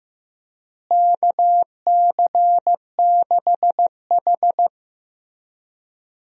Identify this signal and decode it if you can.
Morse code, heard as a keyed tone, about 16 wpm, 700 Hz. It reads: KC6H
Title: USA Call Signs - 4 Characters - Repeated 2x - 15wpm